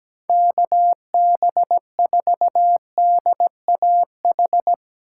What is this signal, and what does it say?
Morse code, heard as a keyed tone, about 17 wpm, 700 Hz. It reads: KB4DAH
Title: Speed Racing - USA 2x3 Call Signs 17wpm